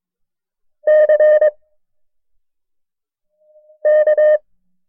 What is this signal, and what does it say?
Morse code, heard as a keyed tone, about 22 wpm, 610 Hz. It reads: C K
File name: FreeFiles / CW-practice / KB6MT-tape1-side2